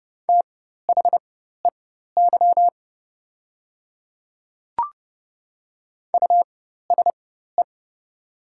Morse code, heard as a keyed tone, about 30 wpm, spaced as for 14 wpm, 700 Hz. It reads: THEY USE
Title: 4x Character Spacing - Top 200 Words - 30wpm